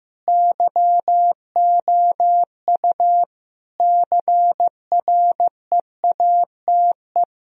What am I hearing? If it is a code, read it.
Morse code, heard as a keyed tone, about 15 wpm, 700 Hz. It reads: YOU CREATE
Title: Sentences from Top 1000 Words - Review Part 3 - 15wpm